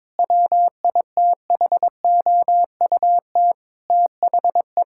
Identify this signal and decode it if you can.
Morse code, heard as a keyed tone, about 21 wpm, 700 Hz. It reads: WITHOUT THE
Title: Sentences from Top 1000 Words - Review Part 6 - 22wpm